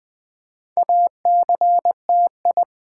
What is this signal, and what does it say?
Morse code, heard as a keyed tone, about 20 wpm, 700 Hz. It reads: ACTI
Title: Build A Word - 2 Letters At A Time - 8-Letter Words - 20wpm